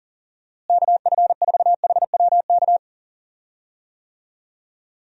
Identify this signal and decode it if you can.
Morse code, heard as a keyed tone, about 39 wpm, 700 Hz. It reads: KF4HWK
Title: USA Call Signs - 6 Characters - Repeated 2x - 40wpm